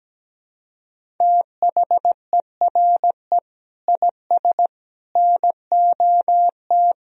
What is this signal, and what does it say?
Morse code, heard as a keyed tone, about 17 wpm, 700 Hz. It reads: THERE IS NOT
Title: Sentences from Top 1000 Words - Review Part 1 - 17wpm